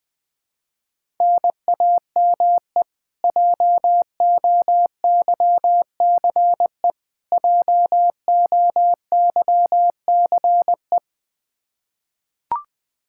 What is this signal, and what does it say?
Morse code, heard as a keyed tone, about 21 wpm, 700 Hz. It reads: NAME JOYCE JOYCE
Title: Practice QSO - Exchange 1 - Beginning - 20wpm